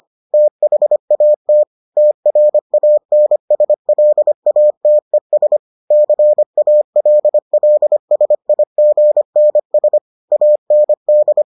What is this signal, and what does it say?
Morse code, heard as a keyed tone, about 25 wpm, 600 Hz. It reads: THAT TRANSLATES CALLSIGNS AND